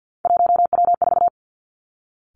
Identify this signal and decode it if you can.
Morse code, heard as a keyed tone, about 50 wpm, 700 Hz. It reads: 1W4
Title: Three Challenging Letter-Number Combinations 50wpm